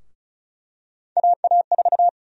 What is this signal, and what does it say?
Morse code, heard as a keyed tone, about 36 wpm, 700 Hz. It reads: AA4
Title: USA Call Sign Prefixes - 35wpm